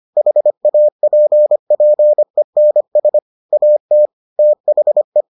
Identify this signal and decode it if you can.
Morse code, heard as a keyed tone, about 25 wpm, 600 Hz. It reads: HAPPENS AT THE